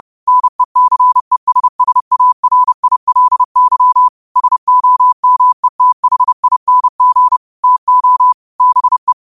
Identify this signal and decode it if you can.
Morse code, heard as a keyed tone, about 30 wpm, 1000 Hz. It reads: NECESSARILY SOMETHING TO BE